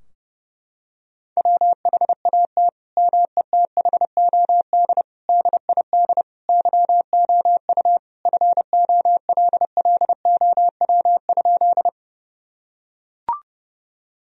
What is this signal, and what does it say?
Morse code, heard as a keyed tone, about 30 wpm, 700 Hz. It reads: WHAT METHOD DID YOU FOLLOW?
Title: Sentences from Top 1000 Words - Review Part 3 - 30wpm